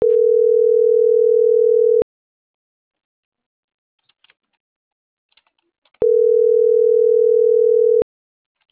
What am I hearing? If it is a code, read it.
Ringback tone (North America), 440 + 480 Hz.
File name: CENTRAL MAYAB / out esmeralda.garcia